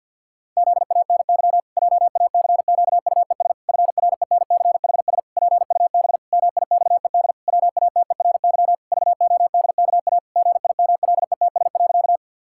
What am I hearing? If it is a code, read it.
Morse code, heard as a keyed tone, about 50 wpm, 700 Hz. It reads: ZANY JACQUES FRENCHS PUB MIXED WATERY VODKA GIMLETS.
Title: Pangram Challenge v2 - 50wpm